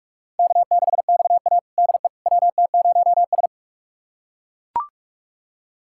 Morse code, heard as a keyed tone, about 41 wpm, 700 Hz. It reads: K6XA DE WT0S
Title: USA Callsign DE Callsign - 4 Characters - 45wpm